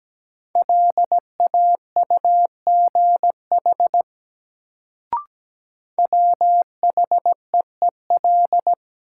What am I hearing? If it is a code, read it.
Morse code, heard as a keyed tone, about 17 wpm, 700 Hz. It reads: LAUGH WHEEL
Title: Sentences from Top 500 Words - New Word Drill - 17wpm